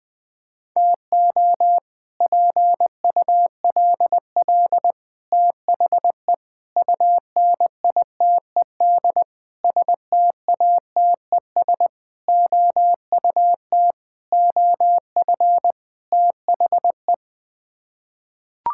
Morse code, heard as a keyed tone, about 20 wpm, 700 Hz. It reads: TO PULL THE UNITED STATES OUT OF THE
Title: Sets of 8 Words - 20wpm